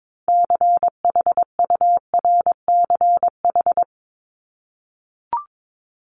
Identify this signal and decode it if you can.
Morse code, heard as a keyed tone, about 22 wpm, 700 Hz. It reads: CHURCH
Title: Top 1751-2000 Words 22wpm